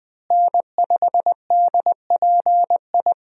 Morse code, heard as a keyed tone, about 21 wpm, 700 Hz. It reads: N5DPI
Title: USA Call Signs - 5 Characters - Repeated 3x - 20wpm